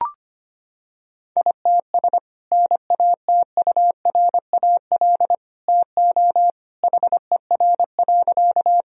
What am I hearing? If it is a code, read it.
Morse code, heard as a keyed tone, about 25 wpm, 700 Hz. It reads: ITS NATURAL TO HER.
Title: Sentences from Top 1000 Words - Review Part 4 - 25wpm